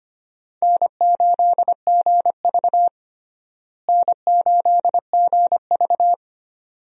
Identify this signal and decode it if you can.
Morse code, heard as a keyed tone, about 25 wpm, 700 Hz. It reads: N8GV N8GV
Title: USA Call Signs - 4 Characters - Repeated 2x - 25wpm